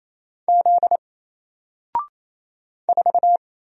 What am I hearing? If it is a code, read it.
Morse code, heard as a keyed tone, about 28 wpm, 700 Hz. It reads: Z 4
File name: Single-Letter-Number-28wpm